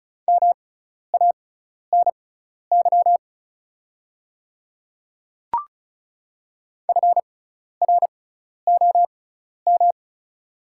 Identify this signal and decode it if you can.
Morse code, heard as a keyed tone, about 35 wpm, spaced as for 12 wpm, 690 Hz. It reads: MANY FROM
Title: 6x Character Spacing - Top 200 Words - 35wpm